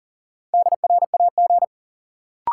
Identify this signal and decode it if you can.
Morse code, heard as a keyed tone, about 40 wpm, 700 Hz. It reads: DRAG
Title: Sentences from Top 1500 Words - New Word Drill - 40wpm